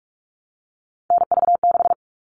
Morse code, heard as a keyed tone, about 44 wpm, 700 Hz. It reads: NV6
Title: All Cut Numbers - 3-Digit 45wpm